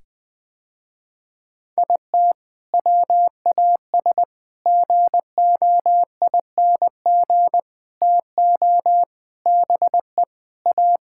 Morse code, heard as a keyed tone, about 19 wpm, 700 Hz. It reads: IT WAS GOING TO BE A
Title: Sentences with 5-Word Phrases - Spoken Beginning - 20wpm